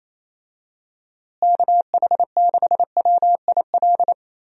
Morse code, heard as a keyed tone, about 28 wpm, 700 Hz. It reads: KH6WIL
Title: USA Call Signs - 6 Characters - Repeated 3x - 28wpm